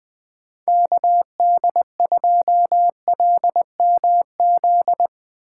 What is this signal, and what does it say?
Morse code, heard as a keyed tone, about 20 wpm, 700 Hz. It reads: KD2LMZ
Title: Speed Racing - USA Call Signs 17wpm